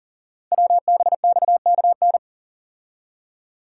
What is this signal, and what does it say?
Morse code, heard as a keyed tone, about 40 wpm, 700 Hz. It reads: WBXKN
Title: Code Groups - 5 Letters - 40wpm